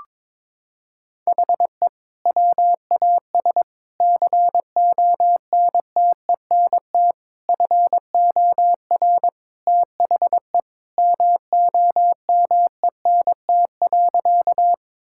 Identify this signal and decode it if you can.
Morse code, heard as a keyed tone, about 22 wpm, 700 Hz. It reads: HE WAS CONTENT FOR THE MOMENT.
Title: Sentences from Top 1000 Words - Review Part 6 - 22wpm